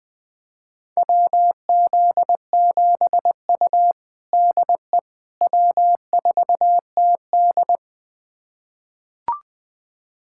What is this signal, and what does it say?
Morse code, heard as a keyed tone, about 21 wpm, 700 Hz. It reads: WZ7U DE W4TD
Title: USA Callsign DE Callsign - 4 Characters - 20wpm